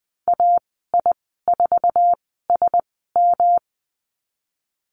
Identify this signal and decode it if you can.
Morse code, heard as a keyed tone, about 20 wpm, spaced as for 15 wpm, 700 Hz. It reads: AI4SM
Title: 154 Lesson 25g - Letter p - Call Signs 20wpm